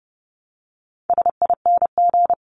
Morse code, heard as a keyed tone, about 30 wpm, 700 Hz. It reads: SING
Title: Top 500 Words Repeated 30wpm